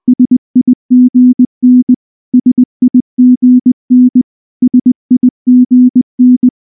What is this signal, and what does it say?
Morse code, heard as a keyed tone, about 19 wpm, 250 Hz. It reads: SIGN SIGN SIGN
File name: morse / morse-training-20WPM-1